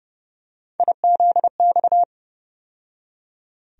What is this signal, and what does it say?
Morse code, heard as a keyed tone, about 30 wpm, 700 Hz. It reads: IZX
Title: USA Call Sign Postfixes - 3 Letters 30wpm